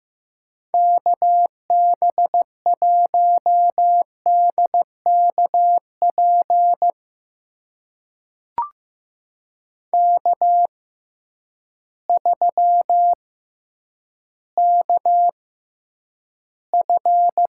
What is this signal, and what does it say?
Morse code, heard as a keyed tone, about 15 wpm, spaced as for 9 wpm, 700 Hz. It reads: KB1DKP K3KF
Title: USA Call Signs - 6x Character Spacing - 15wpm